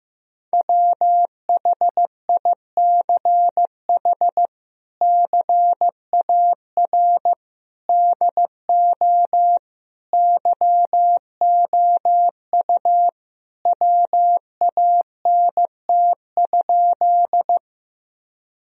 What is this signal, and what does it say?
Morse code, heard as a keyed tone, about 15 wpm, 700 Hz. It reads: WHICH CAR DO YOU WANT?
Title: Sentences from Top 1000 Words - Review Part 1 - 15wpm